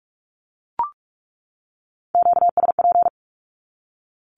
Morse code, heard as a keyed tone, about 44 wpm, 700 Hz. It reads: QSP
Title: Q-Codes 45wpm